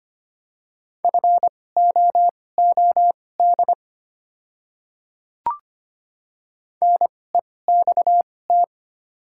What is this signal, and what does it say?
Morse code, heard as a keyed tone, about 25 wpm, spaced as for 18 wpm, 700 Hz. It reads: FOOD NEXT
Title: 2x Character Spacing - Top 500 Words - 25wpm